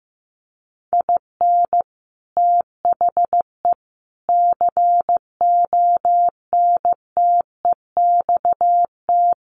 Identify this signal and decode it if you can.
Morse code, heard as a keyed tone, about 15 wpm, 700 Hz. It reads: IN THE CONTEXT
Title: Sets of 5 Words 15wpm